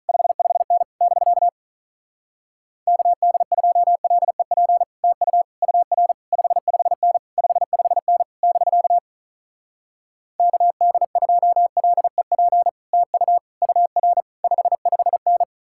Morse code, heard as a keyed tone, about 46 wpm, 700 Hz. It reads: 55N <BK> KD2LEP TU UR 55N 55N <BK> KD2LEP TU UR 55N